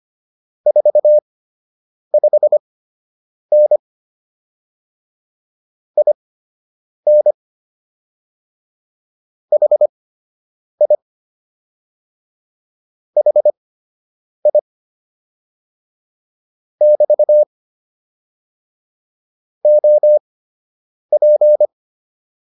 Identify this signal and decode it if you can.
Morse code, heard as a keyed tone, about 25 wpm, spaced as for 8 wpm, 600 Hz. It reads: 45N IN HI HI <BT> OP